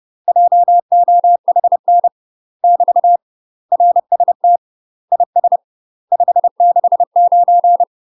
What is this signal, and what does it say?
Morse code, heard as a keyed tone, about 30 wpm, 700 Hz. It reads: JOHN <BT> RST IS 569
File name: UK MASTER QSOs TRAINING / 30WPM / UK MASTER QSOs 0011